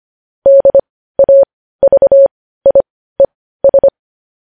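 Morse code, heard as a keyed tone, about 25 wpm, spaced as for 15 wpm, 550 Hz. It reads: DAVIES
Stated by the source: Morse news headlines practice, possibly sent at 15 wpm.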